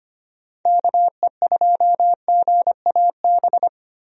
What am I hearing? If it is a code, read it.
Morse code, heard as a keyed tone, about 25 wpm, 700 Hz. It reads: KE2GAB
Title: USA Call Signs - 6 Characters - 25wpm